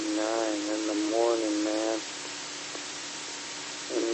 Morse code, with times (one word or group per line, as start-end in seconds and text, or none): none